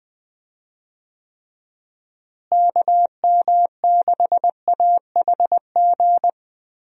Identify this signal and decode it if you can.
Morse code, heard as a keyed tone, about 21 wpm, 700 Hz. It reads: KM6AHG
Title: USA Call Signs - 6 Characters - Repeated 2x - 20wpm